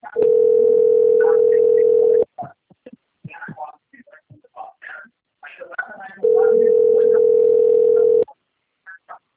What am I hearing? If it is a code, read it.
Ringback tone (North America), 440 + 480 Hz, with speech.
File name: rg-1600-unknown- - - .24512